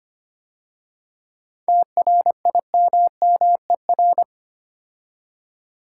Morse code, heard as a keyed tone, about 28 wpm, 690 Hz. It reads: TRIMMER
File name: practice / 25wpm / Difficult Words - Speed-Racing with Hints - 25wpm